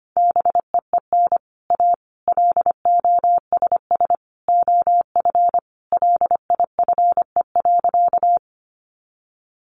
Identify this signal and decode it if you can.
Morse code, heard as a keyed tone, about 25 wpm, 700 Hz. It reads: BEEN A LOSS OF LIFE.
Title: Sentences from Top 1400 Words 25wpm